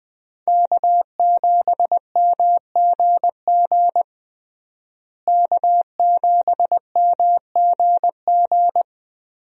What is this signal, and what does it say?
Morse code, heard as a keyed tone, about 20 wpm, 700 Hz. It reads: K7MGG K7MGG
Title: USA Call Signs - 5 Characters - Repeated 3x - 20wpm